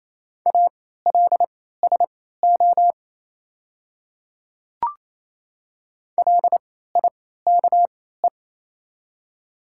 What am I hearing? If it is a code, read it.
Morse code, heard as a keyed tone, about 28 wpm, spaced as for 16 wpm, 700 Hz. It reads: ALSO LIKE
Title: 3x Character Spacing - Top 300 Words - 28wpm